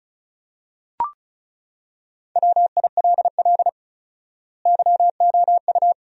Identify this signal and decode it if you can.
Morse code, heard as a keyed tone, about 36 wpm, 700 Hz. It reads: WILL YOU
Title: 4x Word-Spacing - Sentences from Top 200 Words - 35wpm